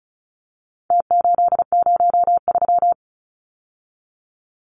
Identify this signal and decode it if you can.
Morse code, heard as a keyed tone, about 34 wpm, 700 Hz. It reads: T803
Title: Numbers - Contest Style - 4-Digit 35wpm